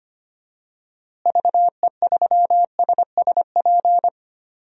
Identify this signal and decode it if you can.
Morse code, heard as a keyed tone, about 25 wpm, 700 Hz. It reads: VE3SSP
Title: DXCC-100 Call Signs - 6 Characters - Repeated 3x - 25wpm